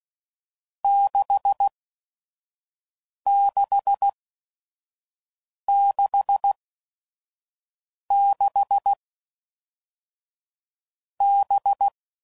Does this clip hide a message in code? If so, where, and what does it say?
Yes: Morse code from 0.8 to 11.9 s, heard as a keyed tone, about 16 wpm, 800 Hz. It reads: 6 6 6 6 B